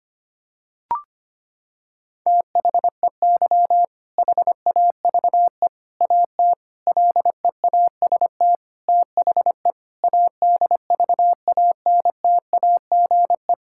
Morse code, heard as a keyed tone, about 25 wpm, 700 Hz. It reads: THEY HAVE AT LEAST THE ADVANTAGE
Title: Sentences from Top 1000 Words - Review Part 6 - 25wpm